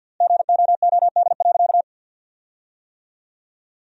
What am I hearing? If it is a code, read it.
Morse code, heard as a keyed tone, about 53 wpm, 700 Hz. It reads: GOOD.
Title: 4x Word-Spacing - Sentences from Top 500 Words - 50wpm